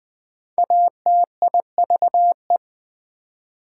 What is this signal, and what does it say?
Morse code, heard as a keyed tone, about 22 wpm, 700 Hz. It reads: ATIVE